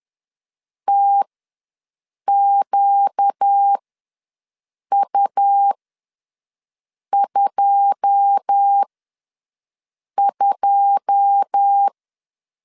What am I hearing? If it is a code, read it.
Morse code, heard as a keyed tone, about 10 wpm, spaced as for 5 wpm, 800 Hz. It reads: TQU22
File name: LEZIONI / 9 2 ws 10wpm 4130